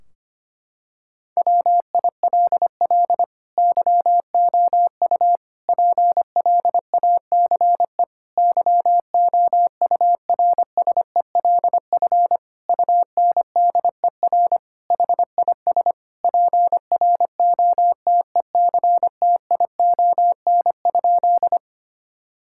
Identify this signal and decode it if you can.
Morse code, heard as a keyed tone, about 25 wpm, 700 Hz. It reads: WILL YOU PLACE YOURSELF UNDER HIS PROTECTION?
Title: Sentences from Top 1600 Words 25wpm